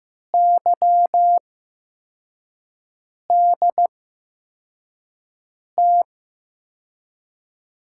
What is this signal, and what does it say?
Morse code, heard as a keyed tone, about 15 wpm, spaced as for 4 wpm, 700 Hz. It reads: YDT
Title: USA Call Signs - 6 Characters - 8x Character Spacing - 15wpm